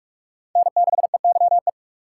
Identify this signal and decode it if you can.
Morse code, heard as a keyed tone, about 41 wpm, 700 Hz. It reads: N6EYE
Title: USA Call Signs - 5 Characters - 4x Character Spacing - 45wpm